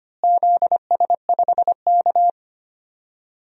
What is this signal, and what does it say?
Morse code, heard as a keyed tone, about 25 wpm, 700 Hz. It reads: ZS5K
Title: DXCC-100 Call Signs - 4 Characters - Repeated 3x - 25wpm